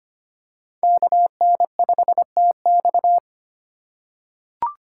Morse code, heard as a keyed tone, about 25 wpm, 700 Hz. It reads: KN5TX
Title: K1USN SST QSO Practice - 25wpm